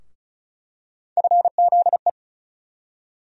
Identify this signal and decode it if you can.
Morse code, heard as a keyed tone, about 36 wpm, 700 Hz. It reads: FZE
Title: USA Call Sign Postfixes - 3 Letters 35wpm